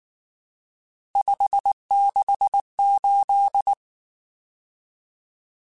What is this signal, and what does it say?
Morse code, heard as a keyed tone, about 19 wpm, 800 Hz. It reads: 568
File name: RSTs / RST-mp3s / Various RST @18